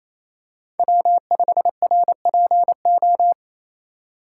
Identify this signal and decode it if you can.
Morse code, heard as a keyed tone, about 28 wpm, 700 Hz. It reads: W5RPO
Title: USA Call Signs - 5 Characters - 2x Character Spacing - 28wpm